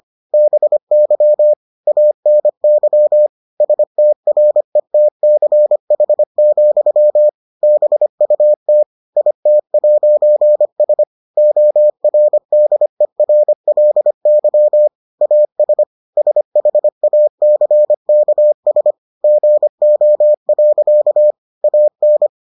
Morse code, heard as a keyed tone, about 25 wpm, 600 Hz. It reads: BY ANY STRETCH, BUT IT'S ORDERLY AS SHACKS GO. AN